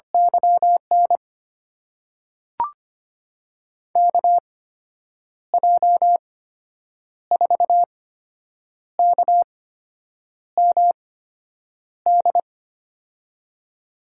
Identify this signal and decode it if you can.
Morse code, heard as a keyed tone, about 25 wpm, spaced as for 8 wpm, 700 Hz. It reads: YN KJ4KMD
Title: USA Call Signs - 6 Characters - 8x Character Spacing - 25wpm